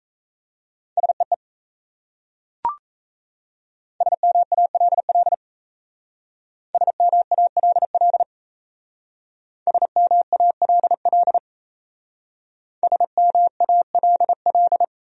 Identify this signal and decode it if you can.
Morse code, heard as a keyed tone, about 37 wpm, 700 Hz. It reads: SEE SMALL SMALL SMALL SMALL